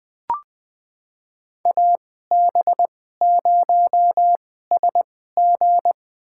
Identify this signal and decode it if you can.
Morse code, heard as a keyed tone, about 20 wpm, spaced as for 15 wpm, 700 Hz. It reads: AB0SG
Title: USA Call Signs - 5 Characters - 2x Character Spacing - 20wpm